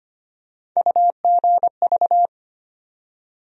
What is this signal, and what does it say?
Morse code, heard as a keyed tone, about 25 wpm, 700 Hz. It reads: UGV
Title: USA Call Sign Postfixes - 3 Letters 25wpm